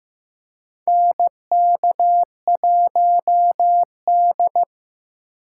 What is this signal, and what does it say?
Morse code, heard as a keyed tone, about 15 wpm, 700 Hz. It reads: NK1D
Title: USA Call Signs - 8x Character Spacing - 15wpm